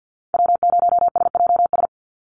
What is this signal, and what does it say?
Morse code, heard as a keyed tone, about 48 wpm, 700 Hz. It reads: W0SJS
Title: Call Signs Repeated 3x - 50wpm